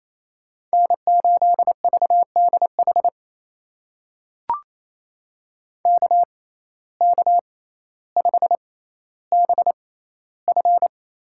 Morse code, heard as a keyed tone, about 28 wpm, spaced as for 14 wpm, 700 Hz. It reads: N8VDH KK5BF
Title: USA Call Signs - 6x Character Spacing - 28wpm